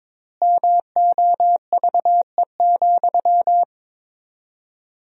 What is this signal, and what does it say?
Morse code, heard as a keyed tone, about 22 wpm, 700 Hz. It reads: MOVE,
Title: 4x Word-Spacing - Sentences from Top 300 Words - 22wpm